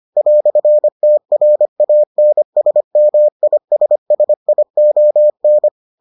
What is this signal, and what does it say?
Morse code, heard as a keyed tone, about 25 wpm, 600 Hz. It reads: "TRANSMISSION